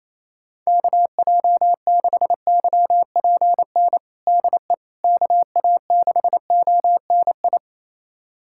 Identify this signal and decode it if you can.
Morse code, heard as a keyed tone, about 28 wpm, 700 Hz. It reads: KJ6YPN DE KA6ONI
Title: USA Callsign DE Callsign - 6 Characters - 28wpm